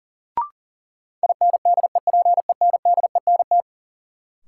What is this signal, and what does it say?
Morse code, heard as a keyed tone, about 40 wpm, 700 Hz. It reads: INDEPENDENT
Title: Sentences from Top 900 Words - New Word Drill - 40wpm